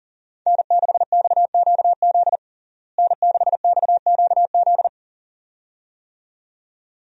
Morse code, heard as a keyed tone, about 40 wpm, 700 Hz. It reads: N6XQZ N6XQZ
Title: USA Call Signs - 5 Characters - Repeated 2x - 40wpm